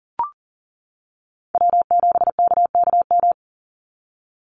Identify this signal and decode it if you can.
Morse code, heard as a keyed tone, about 40 wpm, 700 Hz. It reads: W7KKM
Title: NA SOTA Chaser Callsigns 40wpm